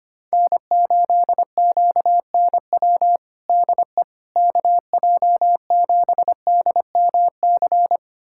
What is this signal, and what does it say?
Morse code, heard as a keyed tone, about 25 wpm, 700 Hz. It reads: N8QNW DE KJ7DMC
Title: USA Callsign DE Callsign 25wpm